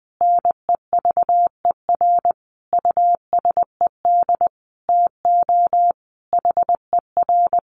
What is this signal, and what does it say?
Morse code, heard as a keyed tone, about 20 wpm, 700 Hz. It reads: NEVER USED TO HER
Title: Sentences from Top 600 Words 20wpm